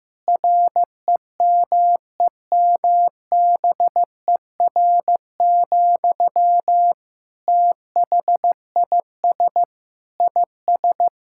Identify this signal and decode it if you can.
Morse code, heard as a keyed tone, about 15 wpm, 700 Hz. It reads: REMEMBER, THIS IS
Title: Sentences from Top 1000 Words - Review Part 5 - 15wpm